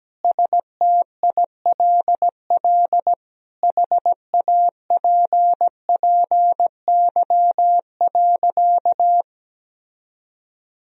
Morse code, heard as a keyed tone, about 17 wpm, 700 Hz. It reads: STILL HAPPY.
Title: Sentences from Top 1000 Words - Review Part 3 - 17wpm